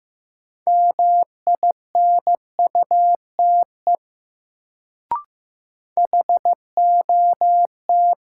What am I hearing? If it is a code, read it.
Morse code, heard as a keyed tone, about 15 wpm, 700 Hz. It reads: MINUTE HOT
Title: Sentences from Top 500 Words - New Word Drill - 15wpm